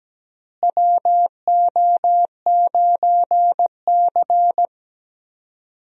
Morse code, heard as a keyed tone, about 17 wpm, 700 Hz. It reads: WO9C
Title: USA Call Signs - 4 Characters - Repeated 3x - 17wpm